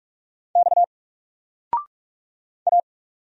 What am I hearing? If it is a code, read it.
Morse code, heard as a keyed tone, about 44 wpm, 700 Hz. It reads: X A
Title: Single Letter-Number Alt 45wpm